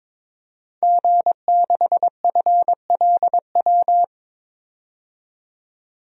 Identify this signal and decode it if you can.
Morse code, heard as a keyed tone, about 22 wpm, 700 Hz. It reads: G6FLW
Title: DXCC-100 Call Signs - 5 Characters - 22wpm